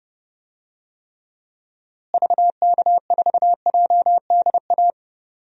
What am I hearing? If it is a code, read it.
Morse code, heard as a keyed tone, about 32 wpm, 700 Hz. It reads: VK4JDA
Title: DXCC-100 Call Signs - 6 Characters - Repeated 3x - 30wpm